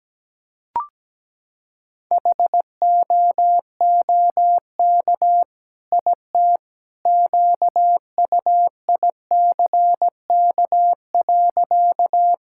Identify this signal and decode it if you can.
Morse code, heard as a keyed tone, about 17 wpm, 700 Hz. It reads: HOOK IT QUICK.
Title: Sentences from Top 1600 Words 17wpm